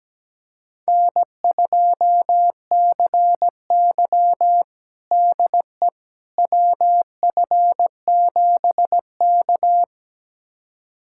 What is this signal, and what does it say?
Morse code, heard as a keyed tone, about 17 wpm, 700 Hz. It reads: N2CY DE WF7K
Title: USA Callsign DE Callsign - 4 Characters - 17wpm